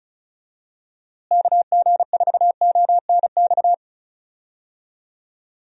Morse code, heard as a keyed tone, about 36 wpm, 700 Hz. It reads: KG4ONX
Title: USA Call Signs - 6 Characters - Repeated 2x - 35wpm